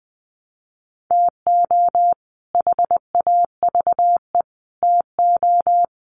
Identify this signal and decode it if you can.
Morse code, heard as a keyed tone, about 20 wpm, 700 Hz. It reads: TO HAVE TO
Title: Sets of 3 Words 20wpm